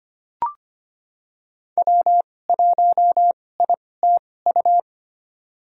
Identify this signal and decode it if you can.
Morse code, heard as a keyed tone, about 25 wpm, spaced as for 18 wpm, 700 Hz. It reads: W1ITU
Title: USA Call Signs - 2x Character Spacing - 25wpm